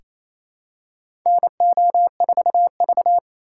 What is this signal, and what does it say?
Morse code, heard as a keyed tone, about 27 wpm, 700 Hz. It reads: NO4V
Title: USA Call Signs - 4 Characters - 4x Character Spacing - 28wpm